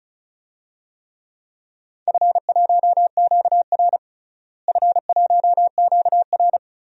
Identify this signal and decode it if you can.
Morse code, heard as a keyed tone, about 36 wpm, 700 Hz. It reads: F1QR F1QR
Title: DXCC-100 Call Signs - 4 Characters - Repeated 3x - 35wpm